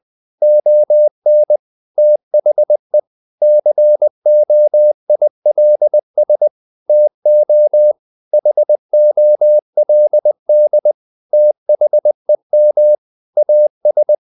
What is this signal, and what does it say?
Morse code, heard as a keyed tone, about 20 wpm, 600 Hz. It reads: ON THE COILS TO HOLD THEM AS
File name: the old man and the sea / the old man and the sea part 1 20wpm